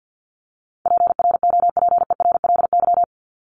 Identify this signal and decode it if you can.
Morse code, heard as a keyed tone, about 49 wpm, 700 Hz. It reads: PROPERLY
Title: Top 1001-1250 Words 50wpm